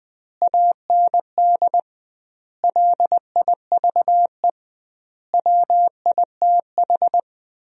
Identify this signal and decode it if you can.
Morse code, heard as a keyed tone, about 20 wpm, 700 Hz. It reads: AND LIVE WITH
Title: 2x Word-Spacing - Sentences from Top 500 Words - 20wpm